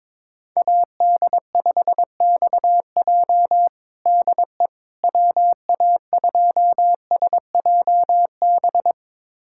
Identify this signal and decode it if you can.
Morse code, heard as a keyed tone, about 22 wpm, 700 Hz. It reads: AD5XJ DE WA2SJB
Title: USA Callsign DE Callsign 22wpm